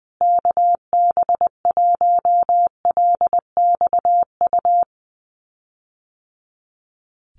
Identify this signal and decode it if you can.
Morse code, heard as a keyed tone, about 20 wpm, 700 Hz. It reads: KB1LXU
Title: USA Call Signs Repeated 3x 20wpm